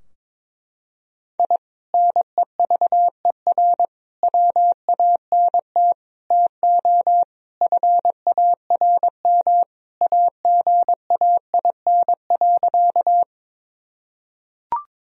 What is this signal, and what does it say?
Morse code, heard as a keyed tone, about 23 wpm, 700 Hz. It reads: I NEVER WANT TO FARM AGAIN.
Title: Sentences from Top 1000 Words - Review Part 2 - 22wpm